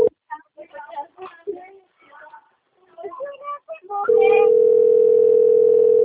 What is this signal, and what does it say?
Ringback tone (North America), 440 + 480 Hz.